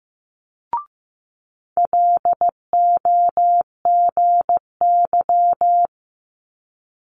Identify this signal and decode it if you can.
Morse code, heard as a keyed tone, about 15 wpm, 700 Hz. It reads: LOGY
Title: Multi-letter Phonemes Quick Fire 15wpm